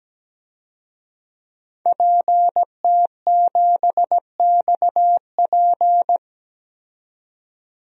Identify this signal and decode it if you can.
Morse code, heard as a keyed tone, about 17 wpm, 700 Hz. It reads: PT7XP
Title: DXCC-100 Call Signs - 5 Characters - 17wpm